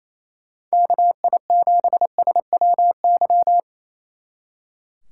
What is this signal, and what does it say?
Morse code, heard as a keyed tone, about 34 wpm, 700 Hz. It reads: KI7SWY
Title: Speed Racing - USA 2x3 Call Signs 28wpm